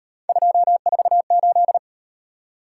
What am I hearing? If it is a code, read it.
Morse code, heard as a keyed tone, about 38 wpm, 700 Hz. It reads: 248